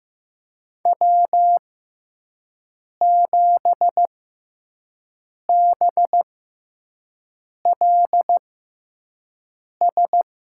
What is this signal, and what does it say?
Morse code, heard as a keyed tone, about 15 wpm, spaced as for 5 wpm, 700 Hz. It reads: W7BLS
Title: USA Call Signs - 6x Character Spacing - 15wpm